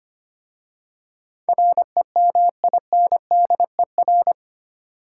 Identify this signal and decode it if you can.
Morse code, heard as a keyed tone, about 25 wpm, 700 Hz. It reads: REMINDER
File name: 8 Letter Words - First 3 Letters Given - 25wpm